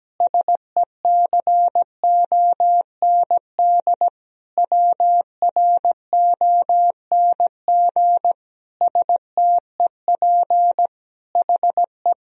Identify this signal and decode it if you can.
Morse code, heard as a keyed tone, about 17 wpm, 700 Hz. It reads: SECOND WRONG STEP HE
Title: Sentences from Top 1000 Words - Review Part 2 - 17wpm